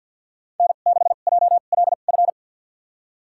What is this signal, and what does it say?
Morse code, heard as a keyed tone, about 52 wpm, spaced as for 37 wpm, 700 Hz. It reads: N6JLF
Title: USA Call Signs - 5 Characters - 2x Character Spacing - 50wpm